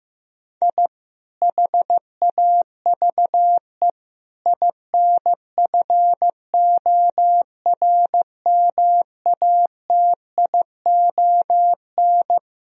Morse code, heard as a keyed tone, about 15 wpm, 700 Hz. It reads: I HAVE INFORMATION
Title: Sentences from Top 1000 Words - Review Part 3 - 15wpm